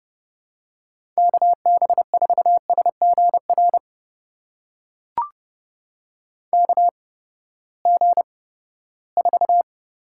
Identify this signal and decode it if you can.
Morse code, heard as a keyed tone, about 30 wpm, spaced as for 17 wpm, 700 Hz. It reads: KB4SGR KG4
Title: USA Call Signs - 6 Characters - 8x Character Spacing - 30wpm